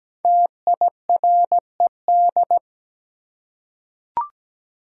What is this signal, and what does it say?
Morse code, heard as a keyed tone, about 17 wpm, 700 Hz. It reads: TIRED
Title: Five Letter Words - 17wpm